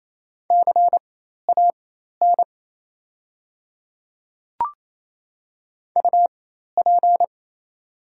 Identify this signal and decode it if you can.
Morse code, heard as a keyed tone, about 28 wpm, spaced as for 13 wpm, 700 Hz. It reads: CAN UP
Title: 4x Character Spacing - Top 100 Words - 28wpm